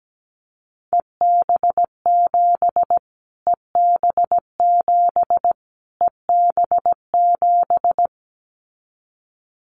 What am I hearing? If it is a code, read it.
Morse code, heard as a keyed tone, about 17 wpm, 700 Hz. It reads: EB7 EB7 EB7
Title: Callsign Prefixes for Top 25 DXCC - Repeated 3x - 17wpm